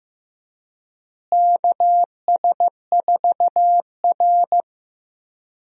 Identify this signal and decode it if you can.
Morse code, heard as a keyed tone, about 15 wpm, 700 Hz. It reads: KS4R
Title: USA Call Signs - 4 Characters - Repeated 3x - 15wpm